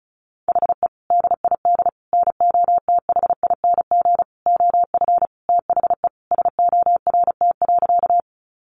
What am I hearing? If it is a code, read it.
Morse code, heard as a keyed tone, about 34 wpm, 700 Hz. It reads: HE DID NOTHING OF THE SORT.
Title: Sentences from Top 1300 Words 35wpm